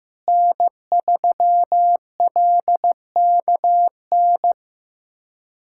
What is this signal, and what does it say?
Morse code, heard as a keyed tone, about 16 wpm, 700 Hz. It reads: N3LKN
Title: USA Callsign DE Callsign - 5 Characters - 15wpm